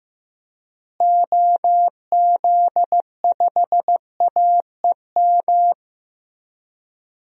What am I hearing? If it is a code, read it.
Morse code, heard as a keyed tone, about 16 wpm, 700 Hz. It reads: OZ5AEM
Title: DXCC-100 Call Signs - 6 Characters - Repeated 3x - 15wpm